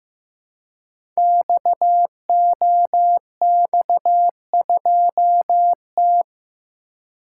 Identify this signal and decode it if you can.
Morse code, heard as a keyed tone, about 15 wpm, 700 Hz. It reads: XOX2T
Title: Code Groups - 5 Letter:Numbers - 15wpm